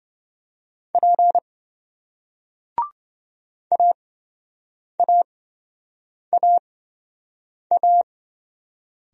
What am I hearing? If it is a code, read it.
Morse code, heard as a keyed tone, about 27 wpm, spaced as for 7 wpm, 700 Hz. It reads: P AAAA